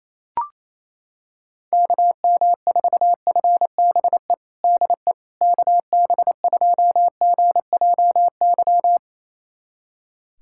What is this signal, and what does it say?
Morse code, heard as a keyed tone, about 27 wpm, 700 Hz. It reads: KM4FBE DE KB2GJY
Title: USA Callsign DE Callsign - 6 Characters - 28wpm